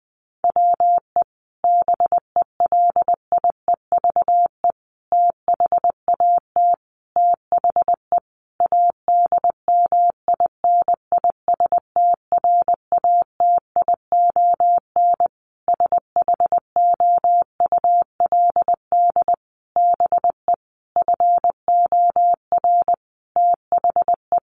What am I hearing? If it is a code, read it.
Morse code, heard as a keyed tone, about 20 wpm, 700 Hz. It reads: WE BELIEVE THAT THE ADMINISTRATION SHOULD BE FOR THE
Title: Sentences from Top 1300 Words 20wpm